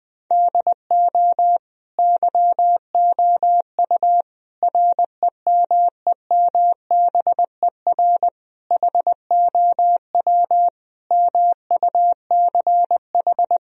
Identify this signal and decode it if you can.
Morse code, heard as a keyed tone, about 20 wpm, 700 Hz. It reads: DO YOU REMEMBER HOW MUCH
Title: Sentences from Top 1000 Words - Review Part 3 - 20wpm